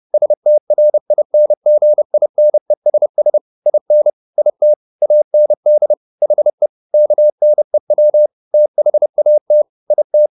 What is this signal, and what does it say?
Morse code, heard as a keyed tone, about 30 wpm, 600 Hz. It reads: STRINGINESS IN IT AND HE KNEW THAT IT